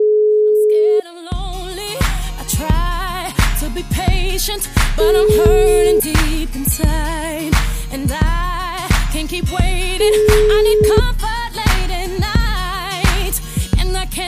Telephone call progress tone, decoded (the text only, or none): ringback tone